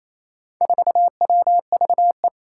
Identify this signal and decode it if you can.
Morse code, heard as a keyed tone, about 27 wpm, 700 Hz. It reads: 4WVE
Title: Four Challenging Letter-Number Combinations 28wpm